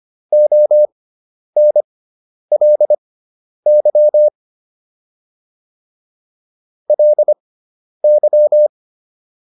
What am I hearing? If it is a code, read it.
Morse code, heard as a keyed tone, about 25 wpm, spaced as for 10 wpm, 600 Hz. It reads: ONLY LY